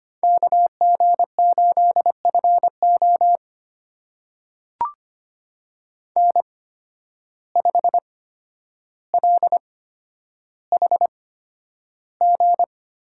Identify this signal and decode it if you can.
Morse code, heard as a keyed tone, about 25 wpm, spaced as for 11 wpm, 700 Hz. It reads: KG8FO N5LHG
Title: USA Call Signs - 5 Characters - 8x Character Spacing - 25wpm